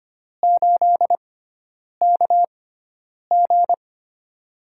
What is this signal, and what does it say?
Morse code, heard as a keyed tone, about 25 wpm, spaced as for 9 wpm, 700 Hz. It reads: 8KG
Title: USA Call Signs - 4 Characters - 6x Character Spacing - 25wpm